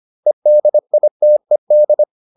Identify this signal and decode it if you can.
Morse code, heard as a keyed tone, about 25 wpm, 600 Hz. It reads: EDITED